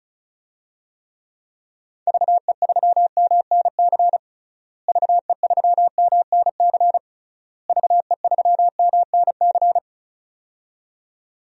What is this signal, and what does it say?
Morse code, heard as a keyed tone, about 34 wpm, 700 Hz. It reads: VE3MNC VE3MNC VE3MNC
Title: DXCC-100 Call Signs - 6 Characters - Repeated 3x - 35wpm